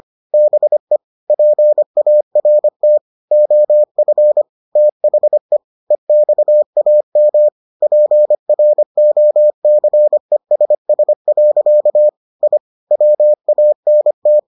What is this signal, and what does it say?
Morse code, heard as a keyed tone, about 25 wpm, 600 Hz. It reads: BE PART OF THE EXAM PROCESS. I WANT